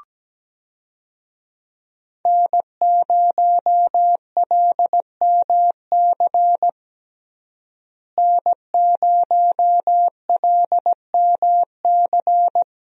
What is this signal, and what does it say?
Morse code, heard as a keyed tone, about 17 wpm, 700 Hz. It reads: N0LMC N0LMC
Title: USA Call Signs - 5 Characters - Repeated 3x - 17wpm